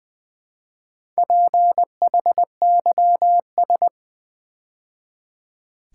Morse code, heard as a keyed tone, about 20 wpm, 700 Hz. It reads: PHYS
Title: Build A Word - 2 Letters At A Time - 8-Letter Words - 20wpm